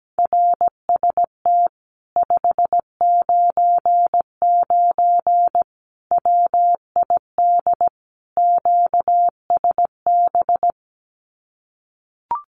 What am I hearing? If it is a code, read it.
Morse code, heard as a keyed tone, about 17 wpm, 700 Hz. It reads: RST 599 WID QSB
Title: Common QSO Phrases Encore 17wpm